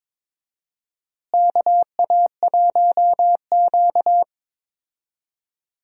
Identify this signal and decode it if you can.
Morse code, heard as a keyed tone, about 22 wpm, 700 Hz. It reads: KA1Q
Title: USA Call Signs - 4 Characters - Repeated 2x - 22wpm